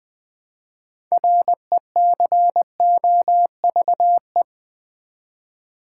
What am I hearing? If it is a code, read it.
Morse code, heard as a keyed tone, about 20 wpm, 700 Hz. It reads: RECOVE
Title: Build A Word - 3 Letters At A Time - 8-Letter Words - 20wpm